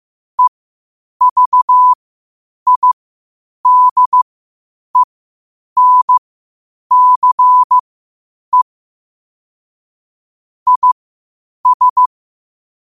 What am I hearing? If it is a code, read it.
Morse code, heard as a keyed tone, about 15 wpm, spaced as for 8 wpm, 1000 Hz. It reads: EVIDENCE IS
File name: qotdfiles / 8wpm